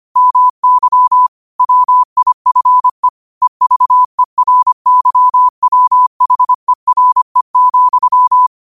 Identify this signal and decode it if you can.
Morse code, heard as a keyed tone, about 25 wpm, 1000 Hz. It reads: MY WIFE EVERYWHERE,
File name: 25wpm